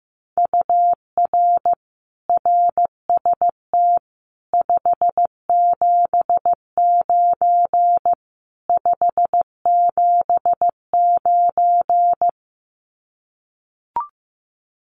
Morse code, heard as a keyed tone, about 15 wpm, 700 Hz. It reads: UR RST 579 579
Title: Common QSO Phrases Encore 15wpm